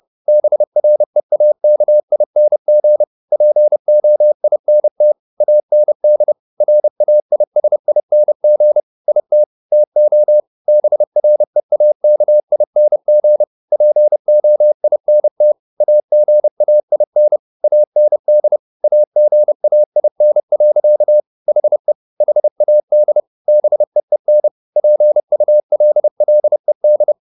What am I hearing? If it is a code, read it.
Morse code, heard as a keyed tone, about 30 wpm, 600 Hz. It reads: BREAKING POINT AND RAISING IT TO BREAKING POINT AGAIN AND AGAIN. HE HAD BEEN PULLED